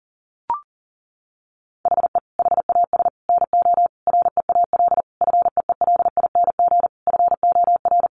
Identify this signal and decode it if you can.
Morse code, heard as a keyed tone, about 40 wpm, 700 Hz. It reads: HE HAS NO REAL FEELING FOR
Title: Sentences from Top 1200 Words 40wpm